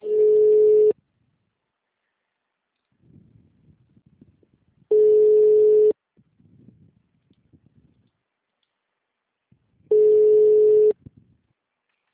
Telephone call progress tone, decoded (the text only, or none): ringback tone